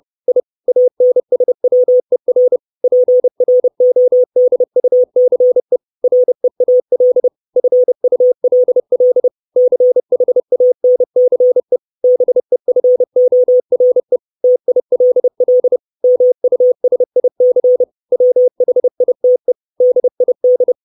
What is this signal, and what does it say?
Morse code, heard as a keyed tone, about 30 wpm, 500 Hz. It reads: I ANSWER PRODUCE REAL FULL CHANCE BEFORE TILL MUSIC WHITE DID
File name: 632 Words 30WPM